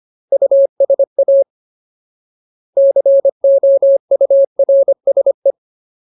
Morse code, heard as a keyed tone, about 25 wpm, 550 Hz. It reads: USA COURSE
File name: Ultimate Word List / 25WPM Ultimate